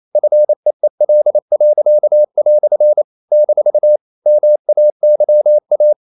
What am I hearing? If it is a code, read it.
Morse code, heard as a keyed tone, about 28 wpm, 600 Hz. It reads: FEEL." - MAYA
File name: CW / Practice / EnglishQuotations0228